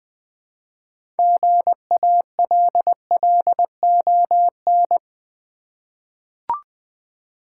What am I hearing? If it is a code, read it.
Morse code, heard as a keyed tone, about 20 wpm, 700 Hz. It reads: GALLON
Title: 6 Letter Words - First 3 Letters Given - 20wpm